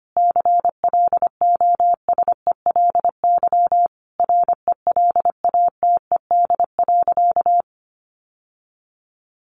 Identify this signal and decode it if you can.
Morse code, heard as a keyed tone, about 25 wpm, 700 Hz. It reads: CLOSELY RELATED.
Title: Sentences from Top 1400 Words 25wpm